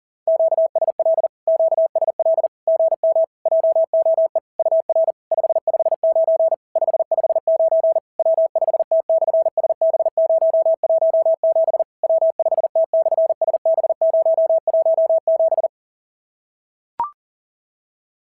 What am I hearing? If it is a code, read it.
Morse code, heard as a keyed tone, about 40 wpm, 690 Hz. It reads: QSL QSL GM JOE UR 559 559 W5T/SB017 W5T/SB017
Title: SOTA QSO Practice 40wpm